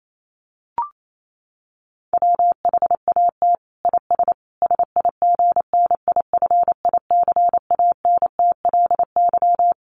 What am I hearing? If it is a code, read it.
Morse code, heard as a keyed tone, about 28 wpm, 700 Hz. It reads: WHAT IS SIGNIFICANTLY
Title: Sentences from Top 1500 Words 28wpm